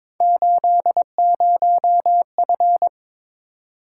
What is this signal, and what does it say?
Morse code, heard as a keyed tone, about 22 wpm, 700 Hz. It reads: 80F
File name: practice / 22wpm / Three Letter-Number Combinations - 22wpm